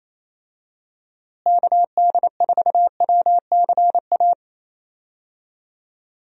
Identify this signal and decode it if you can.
Morse code, heard as a keyed tone, about 28 wpm, 700 Hz. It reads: KD4WCA
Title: USA Call Signs - 6 Characters - Repeated 2x - 28wpm